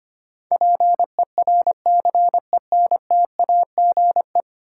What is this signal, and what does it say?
Morse code, heard as a keyed tone, about 25 wpm, 700 Hz. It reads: PERCENTAGE
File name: Ten Letter Words - 25wpm